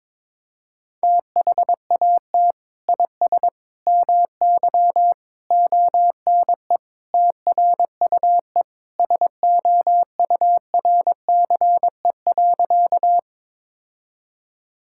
Morse code, heard as a keyed tone, about 22 wpm, 700 Hz. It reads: THAT IS MY ONE TRUE SOURCE.
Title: Sentences from Top 1000 Words - Review Part 5 - 22wpm